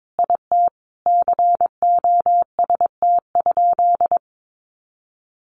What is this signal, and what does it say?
Morse code, heard as a keyed tone, about 22 wpm, 700 Hz. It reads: IT COST?
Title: Sentences from Top 800 Words 22wpm